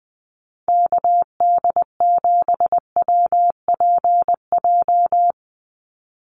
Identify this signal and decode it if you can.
Morse code, heard as a keyed tone, about 20 wpm, 700 Hz. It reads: KD7WPJ
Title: NA SOTA Activator Callsigns 20wpm